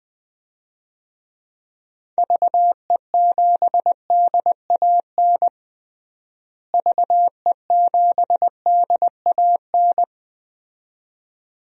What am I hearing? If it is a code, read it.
Morse code, heard as a keyed tone, about 20 wpm, 700 Hz. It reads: VE7DAN VE7DAN
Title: DXCC-100 Call Signs - 6 Characters - Repeated 2x - 20wpm